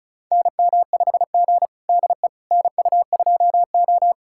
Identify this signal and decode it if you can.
Morse code, heard as a keyed tone, about 36 wpm, 700 Hz. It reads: NM5G DE NU2O
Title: USA Callsign DE Callsign - 4 Characters - 35wpm